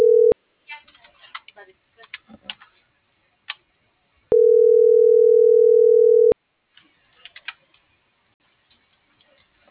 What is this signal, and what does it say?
Ringback tone (North America), 440 + 480 Hz.